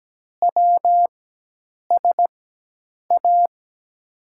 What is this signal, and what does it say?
Morse code, heard as a keyed tone, about 17 wpm, spaced as for 8 wpm, 700 Hz. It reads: WSA
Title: USA Call Signs - 6 Characters - 4x Character Spacing - 17wpm